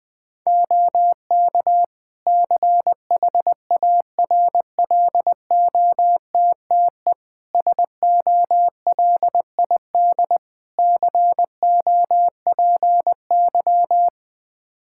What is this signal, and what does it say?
Morse code, heard as a keyed tone, about 21 wpm, 700 Hz. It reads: OK CHARLOTTE SOLID COPY
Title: Practice QSO - Exchange 3 - More Info - 20wpm